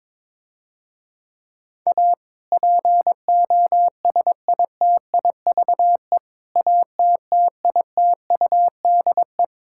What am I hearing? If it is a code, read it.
Morse code, heard as a keyed tone, about 21 wpm, 700 Hz. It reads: A POSITIVE ATTITUDE
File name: Difficult Sentences with Hints - 22wpm